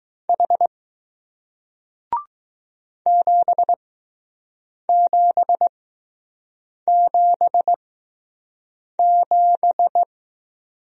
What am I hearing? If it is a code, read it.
Morse code, heard as a keyed tone, about 20 wpm, spaced as for 6 wpm, 700 Hz. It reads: H 7777